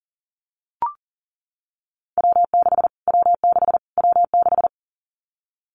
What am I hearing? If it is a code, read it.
Morse code, heard as a keyed tone, about 40 wpm, 700 Hz. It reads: W6 W6 W6
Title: Callsign Prefixes for Top 25 DXCC - Repeated 3x - 40wpm